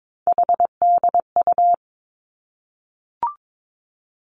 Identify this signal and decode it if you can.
Morse code, heard as a keyed tone, about 22 wpm, 700 Hz. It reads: HDU
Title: Three Challenging Letter-Number Combinations 22wpm